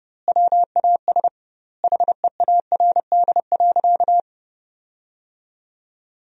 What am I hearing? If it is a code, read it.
Morse code, heard as a keyed tone, about 30 wpm, 700 Hz. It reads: WAS HEARD.
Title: 2x Word-Spacing - Sentences from Top 400 Words - 30wpm